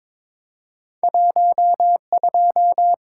Morse code, heard as a keyed tone, about 22 wpm, 700 Hz. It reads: 12
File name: Two Letter-Number Combinations - 22wpm